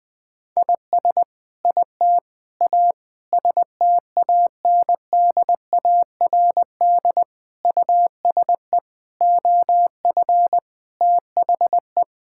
Morse code, heard as a keyed tone, about 20 wpm, 700 Hz. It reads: IS IT A STANDARD USE OF THE
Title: Sentences from Top 1000 Words - Review Part 6 - 20wpm